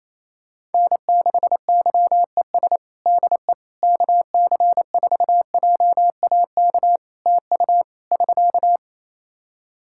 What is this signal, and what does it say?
Morse code, heard as a keyed tone, about 27 wpm, 700 Hz. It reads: N6YES DE KC4JAK TU <SK>
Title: K1USN SST QSO Practice 28wpm